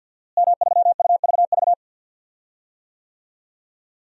Morse code, heard as a keyed tone, about 50 wpm, 700 Hz. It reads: M3UVV
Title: DXCC-100 Call Signs - 5 Characters - Repeated 3x - 50wpm